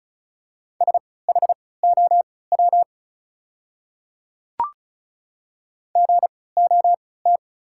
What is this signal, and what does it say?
Morse code, heard as a keyed tone, about 36 wpm, spaced as for 20 wpm, 700 Hz. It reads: SHOW GOT
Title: 3x Character Spacing - Top 500 Words - 35wpm